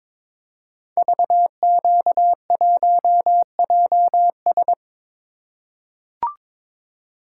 Morse code, heard as a keyed tone, about 22 wpm, 700 Hz. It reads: VQ1JS
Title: Code Groups - 5 Letter:Numbers - 22wpm